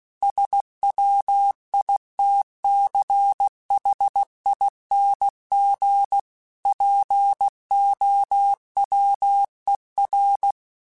Morse code, heard as a keyed tone, about 16 wpm, 800 Hz. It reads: SWITCHING POWER
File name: Technical Subjects / Technical Subjects-mp3s / Sw Pwr Sup @15